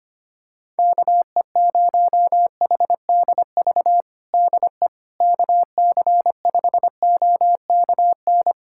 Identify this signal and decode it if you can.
Morse code, heard as a keyed tone, about 25 wpm, 700 Hz. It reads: KE0HDV DE KC5OKN
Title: USA Callsign DE Callsign 25wpm